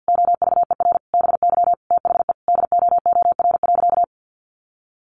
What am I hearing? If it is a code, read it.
Morse code, heard as a keyed tone, about 50 wpm, 700 Hz. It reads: OVER BY THE DOOR.
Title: Sentences from Top 1400 Words 50wpm